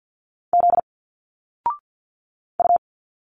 Morse code, heard as a keyed tone, about 51 wpm, 700 Hz. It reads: Z U
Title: Single Letters 50wpm